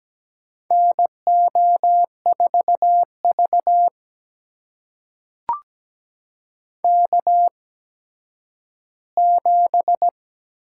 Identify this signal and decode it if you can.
Morse code, heard as a keyed tone, about 17 wpm, spaced as for 10 wpm, 700 Hz. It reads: NO4V K7
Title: USA Call Signs - 4 Characters - 8x Character Spacing - 17wpm